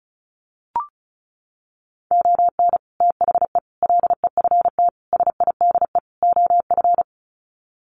Morse code, heard as a keyed tone, about 34 wpm, 700 Hz. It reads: ON THE LEFT SIDE OF
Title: Sets of 5 Words Encore2 35wpm